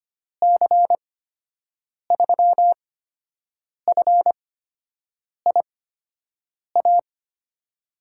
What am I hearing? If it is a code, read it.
Morse code, heard as a keyed tone, about 25 wpm, spaced as for 7 wpm, 700 Hz. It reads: C3FIA
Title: USA Call Signs - 6 Characters - 8x Character Spacing - 25wpm